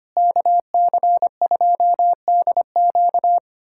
Morse code, heard as a keyed tone, about 25 wpm, 700 Hz. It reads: KC2DQ
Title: USA Callsign DE Callsign - 5 Characters - 25wpm